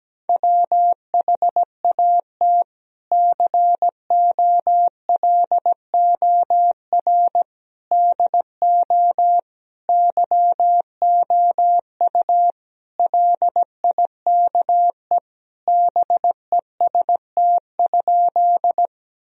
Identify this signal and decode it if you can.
Morse code, heard as a keyed tone, about 17 wpm, 700 Hz. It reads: WHAT COLOR DO YOU LIKE BEST?
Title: Sentences from Top 1000 Words - Review Part 2 - 17wpm